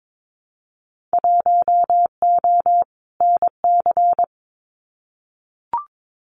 Morse code, heard as a keyed tone, about 22 wpm, 700 Hz. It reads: 1O NC
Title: Winter Field Day Exchanges 22wpm